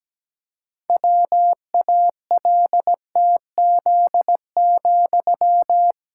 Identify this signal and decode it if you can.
Morse code, heard as a keyed tone, about 17 wpm, 700 Hz. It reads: WALTZ,
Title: Pangram Challenge v2 - 17wpm